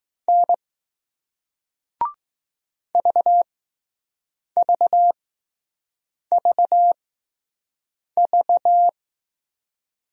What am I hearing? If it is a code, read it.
Morse code, heard as a keyed tone, about 20 wpm, spaced as for 6 wpm, 700 Hz. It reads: N VVVV